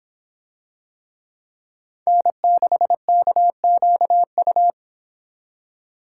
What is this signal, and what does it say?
Morse code, heard as a keyed tone, about 25 wpm, 700 Hz. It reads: N6KQU